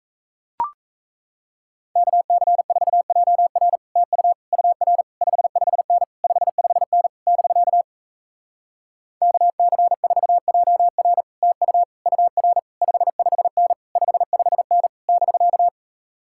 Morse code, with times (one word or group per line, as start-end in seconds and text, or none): 2.0-3.8 KC4JR
4.0-4.3 TU
4.5-5.0 UR
5.2-6.0 55N
6.2-7.1 55N
7.3-7.8 <BK>
9.2-11.2 KC4JR
11.4-11.8 TU
12.1-12.6 UR
12.8-13.7 55N
14.0-14.9 55N
15.1-15.7 <BK>